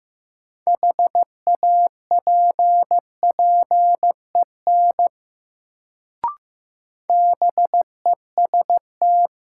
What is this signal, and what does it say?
Morse code, heard as a keyed tone, about 15 wpm, 700 Hz. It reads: HAPPEN BEST
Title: Sentences from Top 400 Words - New Word Drill - 15wpm